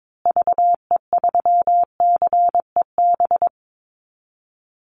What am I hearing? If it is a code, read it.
Morse code, heard as a keyed tone, about 22 wpm, 700 Hz. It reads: VE3CEB
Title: Call Signs Repeated 2x - 22wpm